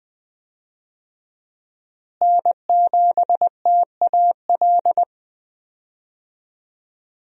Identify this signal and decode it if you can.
Morse code, heard as a keyed tone, about 20 wpm, 700 Hz. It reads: N7TAL
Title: USA Call Signs - 5 Characters - Repeated 2x - 20wpm